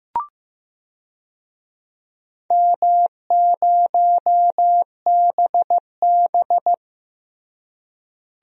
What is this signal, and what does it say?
Morse code, heard as a keyed tone, about 15 wpm, 700 Hz. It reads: M0BB
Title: DXCC-100 Call Signs - 4 Characters - Repeated 3x - 15wpm